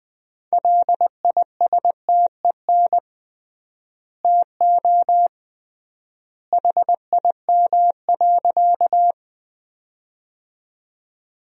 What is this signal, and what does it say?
Morse code, heard as a keyed tone, about 20 wpm, 700 Hz. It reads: LISTEN TO HIM.
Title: 3x Word-Spacing - Sentences from Top 400 Words - 20wpm